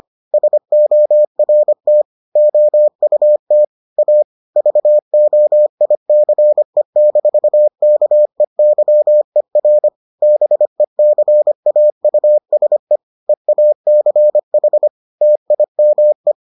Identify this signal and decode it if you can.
Morse code, heard as a keyed tone, about 25 wpm, 600 Hz. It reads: SORT OUT A VOICE-KEYER BECAUSE EACH TIME